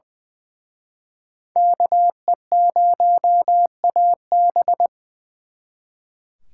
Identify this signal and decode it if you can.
Morse code, heard as a keyed tone, about 24 wpm, 700 Hz. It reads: KE0AB
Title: Speed Racing - USA 2x2 Call Signs 20wpm